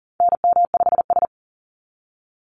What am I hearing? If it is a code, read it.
Morse code, heard as a keyed tone, about 40 wpm, 700 Hz. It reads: NM5S
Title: NA SOTA Activator Callsigns 40wpm